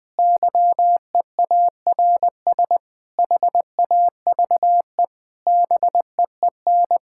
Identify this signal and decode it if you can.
Morse code, heard as a keyed tone, about 20 wpm, 700 Hz. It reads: YEARS HAVE BEEN
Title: Sentences with 5-Word Phrases - Spoken Beginning - 20wpm